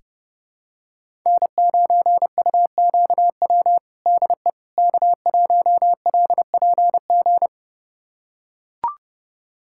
Morse code, heard as a keyed tone, about 30 wpm, 700 Hz. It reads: N9UQW DE K1LPG
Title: USA Callsign DE Callsign - 5 Characters - 30wpm